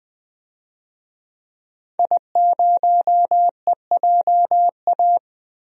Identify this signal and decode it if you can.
Morse code, heard as a keyed tone, about 20 wpm, 700 Hz. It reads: I0EJA
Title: DXCC-100 Call Signs - 5 Characters - Repeated 3x - 20wpm